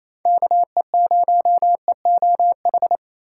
Morse code, heard as a keyed tone, about 28 wpm, 700 Hz. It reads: KE0EOH
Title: USA Call Signs - 6 Characters - 4x Character Spacing - 28wpm